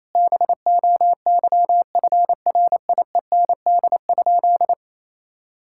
Morse code, heard as a keyed tone, about 28 wpm, 700 Hz. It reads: BOYFRIEND?
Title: Sentences from Top 1000 Words - Review Part 5 - 28wpm